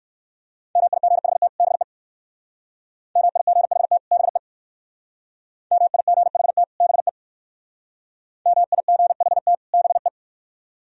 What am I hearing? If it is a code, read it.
Morse code, heard as a keyed tone, about 58 wpm, 700 Hz. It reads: MIGHT BE MIGHT BE MIGHT BE MIGHT BE